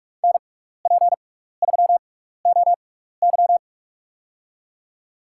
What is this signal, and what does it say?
Morse code, heard as a keyed tone, about 46 wpm, spaced as for 16 wpm, 700 Hz. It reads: NP3OY
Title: USA Call Signs - 5 Characters - 6x Character Spacing - 45wpm